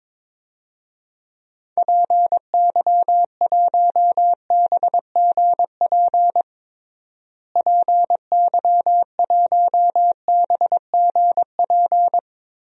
Morse code, heard as a keyed tone, about 22 wpm, 700 Hz. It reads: PY1BGP PY1BGP
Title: DXCC-100 Call Signs - 6 Characters - Repeated 3x - 22wpm